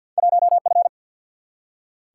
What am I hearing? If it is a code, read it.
Morse code, heard as a keyed tone, about 47 wpm, 700 Hz. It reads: 1F
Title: Single-Letter Single-Number 50wpm